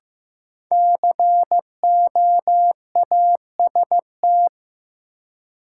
Morse code, heard as a keyed tone, about 15 wpm, 700 Hz. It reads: COAST
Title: Sentences from Top 1000 Words - Review Part 4 - New Word Drill - 15wpm